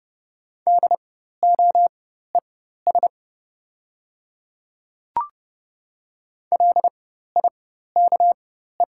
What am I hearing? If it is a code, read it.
Morse code, heard as a keyed tone, about 30 wpm, spaced as for 14 wpm, 700 Hz. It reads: DOES LIKE
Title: 4x Character Spacing - Top 500 Words - 30wpm